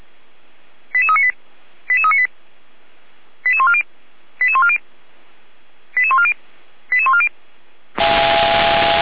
Five-tone selective call (ZVEI), heard as quick runs of five tones, tones of about 70 ms. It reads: 88398 88398 88140 88140 88140 88140